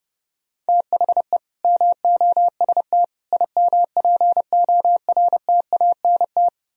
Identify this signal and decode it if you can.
Morse code, heard as a keyed tone, about 38 wpm, 700 Hz. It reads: THE MOST IMPORTANT
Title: Speed-Racing - Sets of 4 Words 30wpm